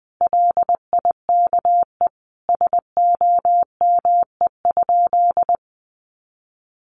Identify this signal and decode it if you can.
Morse code, heard as a keyed tone, about 20 wpm, 700 Hz. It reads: LIKE SOME?
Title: Sentences from Top 100 Words 20wpm